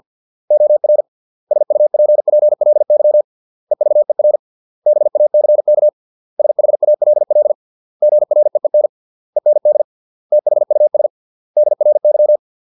Morse code, heard as a keyed tone, about 50 wpm, 600 Hz. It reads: OR SUPPLY EVER BACK SHALL GREEN END THUS DRY